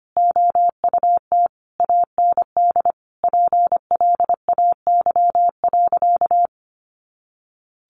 Morse code, heard as a keyed tone, about 25 wpm, 700 Hz. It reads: OUT AND PLAY.
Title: Sentences from Top 200 Words 25wpm